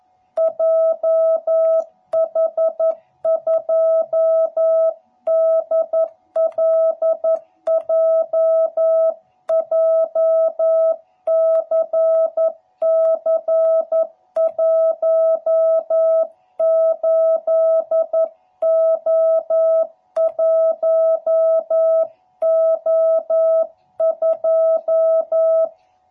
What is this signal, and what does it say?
Morse code, heard as a keyed tone, about 11 wpm, 660 Hz. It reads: JH2DLJ JCC18O1O2
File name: jh8cbh / 2021cw / cw12b